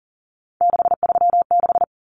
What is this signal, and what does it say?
Morse code, heard as a keyed tone, about 40 wpm, 700 Hz. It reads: 636
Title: Numbers - 3-Digit 40wpm